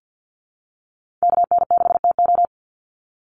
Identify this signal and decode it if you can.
Morse code, heard as a keyed tone, about 50 wpm, 700 Hz. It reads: KN6TO
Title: USA Call Signs 50wpm